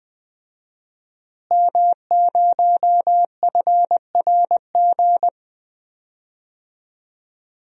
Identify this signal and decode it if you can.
Morse code, heard as a keyed tone, about 21 wpm, 700 Hz. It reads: M0FRG
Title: DXCC-100 Call Signs - 5 Characters - Repeated 2x - 20wpm